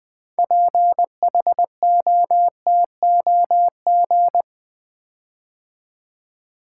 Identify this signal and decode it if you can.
Morse code, heard as a keyed tone, about 20 wpm, 700 Hz. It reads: PHOTOG
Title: Build A Word - 6 Letters At A Time - 12-Letter Words - 20wpm